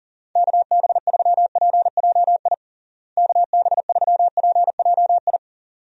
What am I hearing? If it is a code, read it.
Morse code, heard as a keyed tone, about 37 wpm, 700 Hz. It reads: KB3PJI KB3PJI
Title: USA Call Signs - 6 Characters - Repeated 2x - 40wpm